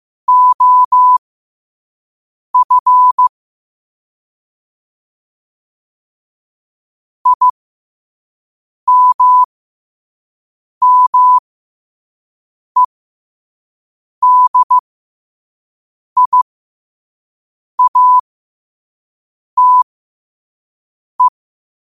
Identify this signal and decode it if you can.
Morse code, heard as a keyed tone, about 15 wpm, spaced as for 5 wpm, 1000 Hz. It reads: OF IMMEDIATE
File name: qotdfiles / 5wpm